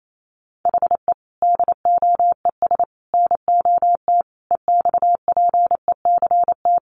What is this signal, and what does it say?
Morse code, heard as a keyed tone, about 28 wpm, 700 Hz. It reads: HE DOES NOT EXPECT
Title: Sentences from Top 1100 Words 28wpm